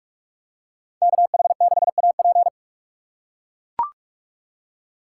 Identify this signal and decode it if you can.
Morse code, heard as a keyed tone, about 44 wpm, 700 Hz. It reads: KH6AP